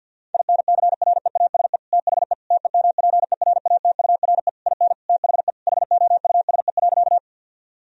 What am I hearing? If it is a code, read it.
Morse code, heard as a keyed tone, about 48 wpm, 700 Hz. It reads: INCREASE THE TEMPERATURE IN THE HOUSE.
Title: Sentences from Top 1000 Words - Review Part 4 - 50wpm